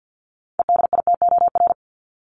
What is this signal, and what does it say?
Morse code, heard as a keyed tone, about 48 wpm, 700 Hz. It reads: EDITOR
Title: Top 1501-1750 Words 50wpm